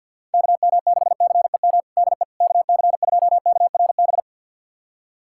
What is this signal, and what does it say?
Morse code, heard as a keyed tone, about 51 wpm, 700 Hz. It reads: KM6CEM DE KC2KRB
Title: USA Callsign DE Callsign - 6 Characters - 50wpm